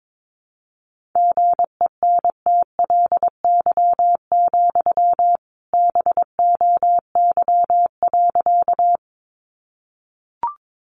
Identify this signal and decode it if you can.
Morse code, heard as a keyed tone, about 22 wpm, 700 Hz. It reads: GENTLY, BOY.
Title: Sentences from Top 1500 Words 22wpm